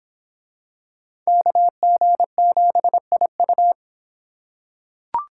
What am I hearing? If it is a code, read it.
Morse code, heard as a keyed tone, about 22 wpm, 700 Hz. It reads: KG7IU